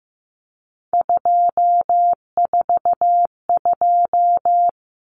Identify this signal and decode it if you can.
Morse code, heard as a keyed tone, about 15 wpm, 700 Hz. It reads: 242
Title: Numbers - 3-Digit 15wpm